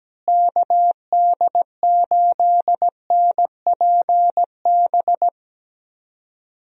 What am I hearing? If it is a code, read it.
Morse code, heard as a keyed tone, about 17 wpm, 700 Hz. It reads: KD8NPB
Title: USA Call Signs - 6 Characters - Repeated 3x - 17wpm